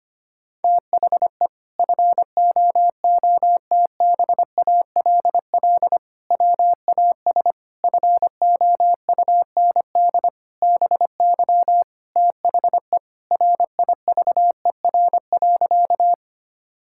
Morse code, heard as a keyed tone, about 25 wpm, 700 Hz. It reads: THE FOOTBALL WAS FOUND BY THE RIVER.
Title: Sentences from Top 1600 Words 25wpm